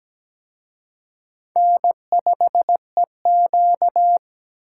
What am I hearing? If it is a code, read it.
Morse code, heard as a keyed tone, about 20 wpm, 700 Hz. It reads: N5EQ
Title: Speed Racing - USA 1x2 Call Signs 17wpm